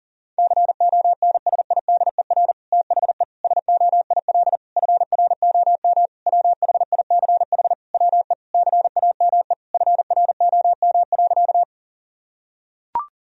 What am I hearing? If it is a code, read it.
Morse code, heard as a keyed tone, about 40 wpm, 700 Hz. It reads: CONSIDER THE SOIL FROM WHICH WE CAME FROM.
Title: Sentences from Top 1000 Words - Review Part 4 - 40wpm